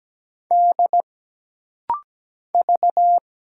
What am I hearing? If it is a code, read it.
Morse code, heard as a keyed tone, about 17 wpm, 700 Hz. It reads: D V
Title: Single Letter-Number - Rapid-Fire 17wpm